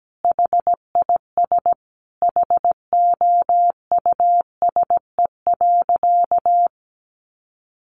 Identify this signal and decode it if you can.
Morse code, heard as a keyed tone, about 17 wpm, 700 Hz. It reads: HIS HOUSE.
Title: Additional Easy Phrases 17wpm